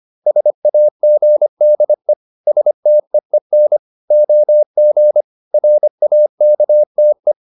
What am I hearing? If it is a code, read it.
Morse code, heard as a keyed tone, about 25 wpm, 600 Hz. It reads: SAGDE STEEN OG RAKTE
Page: Ferie 25 wpm